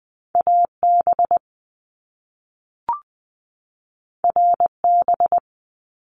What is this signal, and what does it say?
Morse code, heard as a keyed tone, about 19 wpm, 700 Hz. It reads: AB RB
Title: Top Two Letter Combinations in Words 20wpm